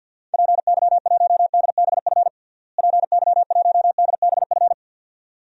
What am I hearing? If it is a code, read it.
Morse code, heard as a keyed tone, about 47 wpm, 700 Hz. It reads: PY1DBF PY1DBF
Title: DXCC-100 Call Signs - 6 Characters - Repeated 2x - 50wpm